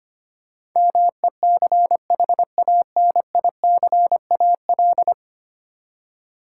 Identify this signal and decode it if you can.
Morse code, heard as a keyed tone, about 25 wpm, 700 Hz. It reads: MECHANICAL
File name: Ten Letter Words - 25wpm